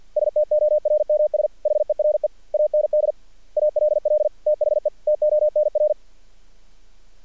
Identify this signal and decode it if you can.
Morse code, heard as a keyed tone, about 48 wpm, 600 Hz. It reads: STORMS HERE AND ALL THE TONR
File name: archived / CRSnet